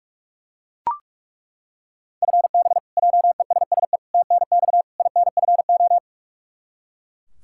Morse code, heard as a keyed tone, about 41 wpm, 700 Hz. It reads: FB JESSE TNX INFO
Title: Practice QSO - Exchange 3 - More Info - 45wpm